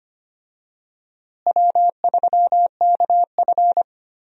Morse code, heard as a keyed tone, about 25 wpm, 700 Hz. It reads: W3KF
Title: USA Call Signs - 4 Characters - Repeated 3x - 25wpm